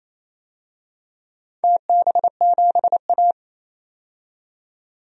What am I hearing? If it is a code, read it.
Morse code, heard as a keyed tone, about 27 wpm, 690 Hz. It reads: TB7A
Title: DXCC-100 Call Signs - 4 Characters - 28wpm